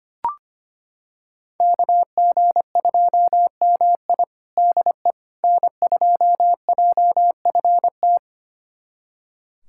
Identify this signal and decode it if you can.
Morse code, heard as a keyed tone, about 25 wpm, 700 Hz. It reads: KG2MI DE N2JFT
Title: USA Callsign DE Callsign 25wpm